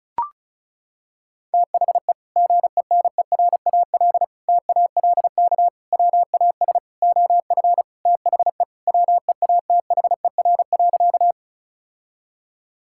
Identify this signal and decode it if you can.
Morse code, heard as a keyed tone, about 36 wpm, 700 Hz. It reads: THE GENERAL TALK WAS OF THE WEATHER.
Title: Sentences from Top 1000 Words - Review Part 5 - 35wpm